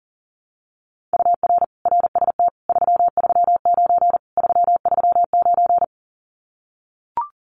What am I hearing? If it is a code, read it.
Morse code, heard as a keyed tone, about 37 wpm, 700 Hz. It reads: UR RST 339 339
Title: Common QSO Phrases Encore 40wpm